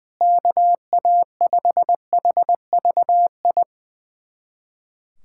Morse code, heard as a keyed tone, about 20 wpm, 700 Hz. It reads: KA5HVI
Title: USA Callsign DE Callsign - 6 Characters - 20wpm